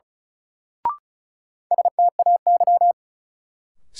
Morse code, heard as a keyed tone, about 36 wpm, 700 Hz. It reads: STAY
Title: Sentences from Top 500 Words - New Word Drill - 35wpm